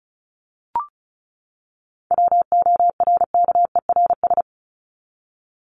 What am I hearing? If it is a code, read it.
Morse code, heard as a keyed tone, about 34 wpm, 700 Hz. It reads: WORKERS
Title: ES:S Suffix - Top 100 Words with Base Words 35wpm